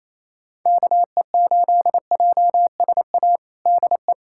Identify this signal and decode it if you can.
Morse code, heard as a keyed tone, about 28 wpm, 700 Hz. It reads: KE8JSA DE
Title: USA Callsign DE Callsign - 6 Characters - 28wpm